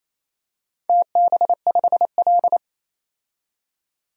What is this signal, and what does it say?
Morse code, heard as a keyed tone, about 28 wpm, 700 Hz. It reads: TB5L
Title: DXCC-100 Call Signs - 4 Characters - 28wpm